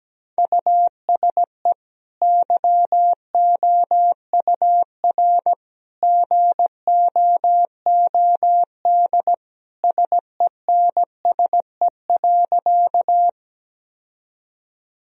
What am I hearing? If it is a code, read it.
Morse code, heard as a keyed tone, about 17 wpm, 700 Hz. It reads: USE YOUR GOOD SENSE.
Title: Sentences from Top 1000 Words - Review Part 4 - 17wpm